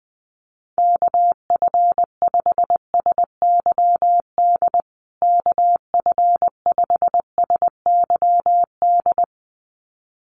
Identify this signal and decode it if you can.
Morse code, heard as a keyed tone, about 20 wpm, 700 Hz. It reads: KF5SYD KF5SYD
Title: USA Call Signs Repeated 2x 20wpm